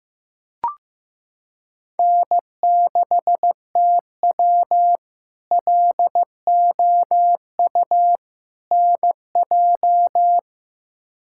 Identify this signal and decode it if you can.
Morse code, heard as a keyed tone, about 15 wpm, 700 Hz. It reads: N6TW LOU NJ
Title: K1USN SST QSO Practice - 15wpm